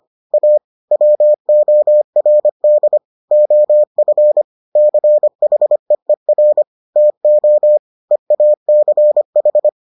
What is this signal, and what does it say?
Morse code, heard as a keyed tone, about 25 wpm, 600 Hz. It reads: A WORD OF CHEER TO EACH